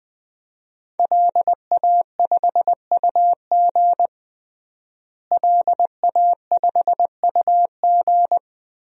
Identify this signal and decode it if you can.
Morse code, heard as a keyed tone, about 20 wpm, 700 Hz. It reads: LA5UG LA5UG
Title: DXCC-100 Call Signs - 5 Characters - Repeated 2x - 20wpm